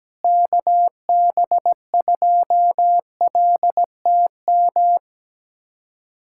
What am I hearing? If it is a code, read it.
Morse code, heard as a keyed tone, about 17 wpm, 700 Hz. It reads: KB2LTM
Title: USA Callsign DE Callsign 17wpm